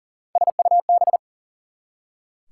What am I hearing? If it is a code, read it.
Morse code, heard as a keyed tone, about 40 wpm, 700 Hz. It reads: SUB
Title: Prefixes and Suffixes 40wpm